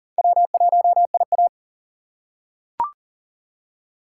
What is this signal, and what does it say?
Morse code, heard as a keyed tone, about 40 wpm, 700 Hz. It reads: W1IA
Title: USA Call Signs - 4 Characters - 6x Character Spacing - 40wpm